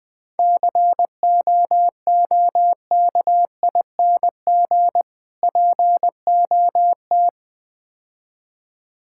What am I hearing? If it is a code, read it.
Morse code, heard as a keyed tone, about 20 wpm, 700 Hz. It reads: COOKING POT
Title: Difficult Words - Speed-Racing with Hints - 20wpm